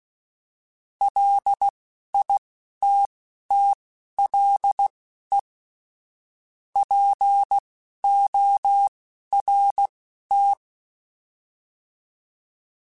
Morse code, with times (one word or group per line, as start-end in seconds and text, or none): 1.0-5.4 LITTLE
6.8-10.5 PORT